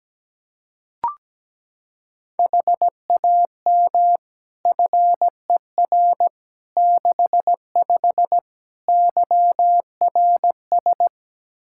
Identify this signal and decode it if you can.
Morse code, heard as a keyed tone, about 17 wpm, 700 Hz. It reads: HAM FER 65 YRS
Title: Practice QSO - Exchange 3 - More Info - 17wpm